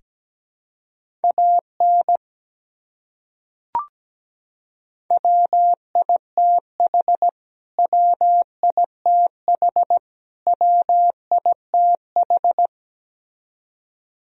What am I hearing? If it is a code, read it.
Morse code, heard as a keyed tone, about 17 wpm, 700 Hz. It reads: AN WITH WITH WITH
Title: Top 100 Words Repeated 17wpm